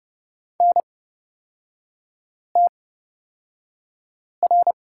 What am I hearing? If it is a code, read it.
Morse code, heard as a keyed tone, about 30 wpm, spaced as for 5 wpm, 700 Hz. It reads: NTR
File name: Lesson 36e - BT - Three Characters 30wpm